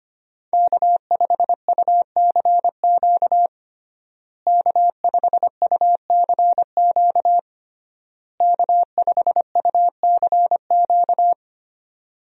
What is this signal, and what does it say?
Morse code, heard as a keyed tone, about 25 wpm, 700 Hz. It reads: K5UCQ K5UCQ K5UCQ
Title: USA Call Signs - 5 Characters - Repeated 3x - 25wpm